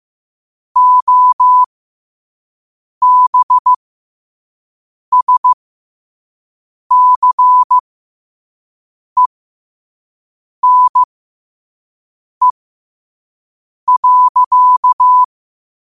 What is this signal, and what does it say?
Morse code, heard as a keyed tone, about 15 wpm, spaced as for 5 wpm, 1000 Hz. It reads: OBSCENE.
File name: qotdfiles / 5wpm